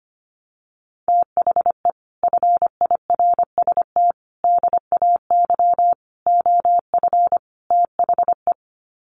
Morse code, heard as a keyed tone, about 25 wpm, 700 Hz. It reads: THE FIRST DAY OF THE
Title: Sets of 5 Words Encore2 25wpm